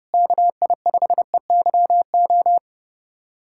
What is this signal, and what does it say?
Morse code, heard as a keyed tone, about 30 wpm, 700 Hz. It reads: KI5EYO
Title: USA Callsign DE Callsign - 6 Characters - 30wpm